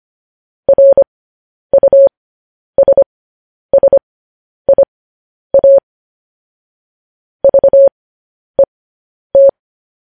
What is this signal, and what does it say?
Morse code, heard as a keyed tone, about 25 wpm, spaced as for 10 wpm, 550 Hz. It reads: RUSSIA VET
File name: podcast / News headlines-1x-25x10